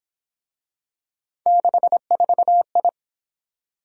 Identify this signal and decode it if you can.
Morse code, heard as a keyed tone, about 27 wpm, 700 Hz. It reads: 64I
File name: Speed-Racing - Three Letter-Number Combinations - 22wpm